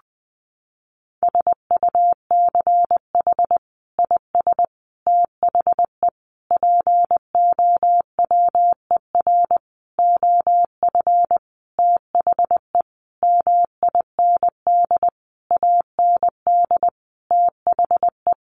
Morse code, heard as a keyed tone, about 20 wpm, 700 Hz. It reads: SUCH IS THE POWER OF THE MIND AND THE
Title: Sentences from Top 1400 Words 20wpm